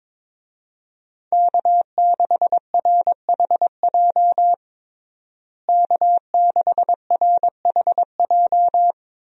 Morse code, heard as a keyed tone, about 22 wpm, 700 Hz. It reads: K6RHJ K6RHJ
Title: USA Call Signs - 5 Characters - Repeated 2x - 22wpm